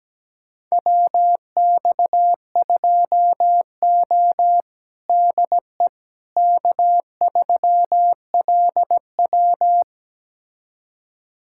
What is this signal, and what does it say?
Morse code, heard as a keyed tone, about 17 wpm, 700 Hz. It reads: WX2O DE K3LW
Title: USA Callsign DE Callsign - 4 Characters - 17wpm